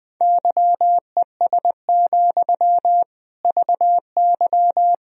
Morse code, heard as a keyed tone, about 20 wpm, 700 Hz. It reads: YES, VY
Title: Sentences from Top 500 Words and Abbreviations 20wpm